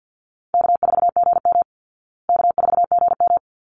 Morse code, heard as a keyed tone, about 48 wpm, 700 Hz. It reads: K4GM K4GM
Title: Easy Call Signs Repeated 2x - 50wpm